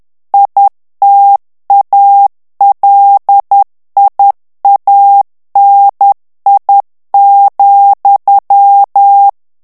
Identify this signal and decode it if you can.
Morse code, heard as a keyed tone, about 10 wpm, 800 Hz. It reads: ITALIANI,
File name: ITALIEN texte clair 2 vitesse 10 mots minute